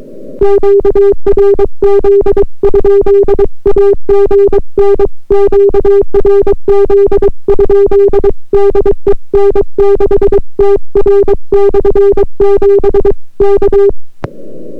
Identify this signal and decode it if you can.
Morse code, heard as a keyed tone, about 22 wpm, 400 Hz. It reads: QRZ? AGN QRZ? DE N6TR/7 K